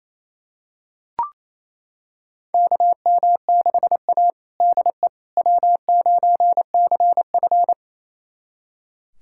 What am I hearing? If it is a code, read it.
Morse code, heard as a keyed tone, about 28 wpm, 700 Hz. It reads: KM6A DE W9CF
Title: USA Callsign DE Callsign - 4 Characters - 28wpm